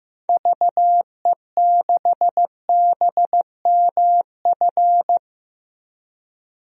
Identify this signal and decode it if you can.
Morse code, heard as a keyed tone, about 15 wpm, 700 Hz. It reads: VE6BMF
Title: DXCC-100 Call Signs - 6 Characters - 15wpm